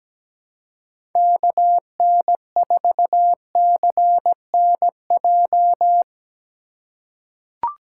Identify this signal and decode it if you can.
Morse code, heard as a keyed tone, about 17 wpm, 700 Hz. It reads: KN4CNJ
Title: USA Call Signs - 6 Characters - 2x Character Spacing - 17wpm